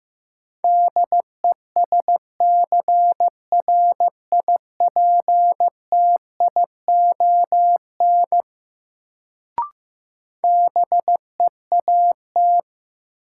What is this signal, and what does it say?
Morse code, heard as a keyed tone, about 15 wpm, 700 Hz. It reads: DESCRIPTION BEAT
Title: Sentences from Top 1300 Words - New Word Drill - 15wpm